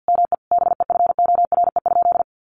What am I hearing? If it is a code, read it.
Morse code, heard as a keyed tone, about 50 wpm, 700 Hz. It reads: ME BEFORE?
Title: Sentences from Top 900 Words 50wpm